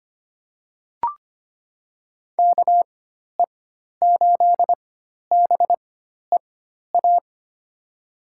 Morse code, heard as a keyed tone, about 25 wpm, spaced as for 12 wpm, 700 Hz. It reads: KE8BEA
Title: USA Call Signs - 4x Character Spacing - 25wpm